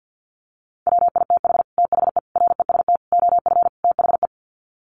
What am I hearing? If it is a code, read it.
Morse code, heard as a keyed tone, about 48 wpm, 700 Hz. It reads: WITH THE REST OF THE
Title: Sets of 5 Words 50wpm